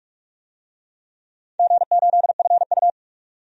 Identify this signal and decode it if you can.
Morse code, heard as a keyed tone, about 46 wpm, 700 Hz. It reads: G8FU
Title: DXCC-100 Call Signs - 4 Characters - 45wpm